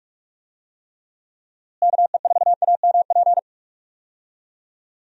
Morse code, heard as a keyed tone, about 46 wpm, 700 Hz. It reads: KE4AMP
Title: USA Call Signs - 6 Characters - Repeated 2x - 45wpm